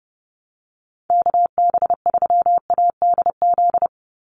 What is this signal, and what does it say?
Morse code, heard as a keyed tone, about 30 wpm, 700 Hz. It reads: KB3ADZ
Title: USA Call Signs 30wpm